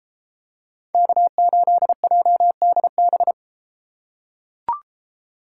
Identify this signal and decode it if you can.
Morse code, heard as a keyed tone, about 31 wpm, 700 Hz. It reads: K8JDB